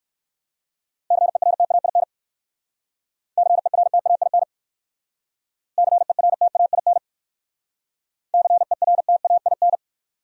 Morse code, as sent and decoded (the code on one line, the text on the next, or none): -.-. .-. - .- .. -. / -.-. .-. - .- .. -. / -.-. .-. - .- .. -. / -.-. .-. - .- .. -.
CERTAIN CERTAIN CERTAIN CERTAIN